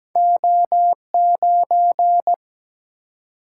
Morse code, heard as a keyed tone, about 17 wpm, 700 Hz. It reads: O 9
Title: Single-Letter Single-Number 17wpm